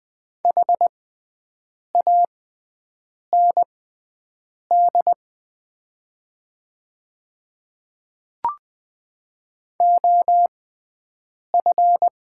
Morse code, heard as a keyed tone, about 20 wpm, spaced as for 7 wpm, 690 Hz. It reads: HAND OF
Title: 6x Character Spacing - Top 200 Words - 20wpm